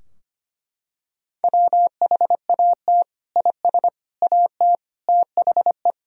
Morse code, heard as a keyed tone, about 25 wpm, 700 Hz. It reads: WHAT IS AT THE
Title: Sentences from Top 1000 Words - Review Part 2 - 25wpm